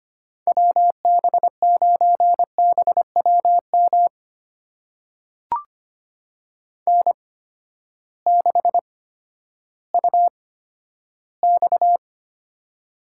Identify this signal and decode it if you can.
Morse code, heard as a keyed tone, about 25 wpm, spaced as for 12 wpm, 700 Hz. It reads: WB9BWM N6UX
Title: USA Call Signs - 8x Character Spacing - 25wpm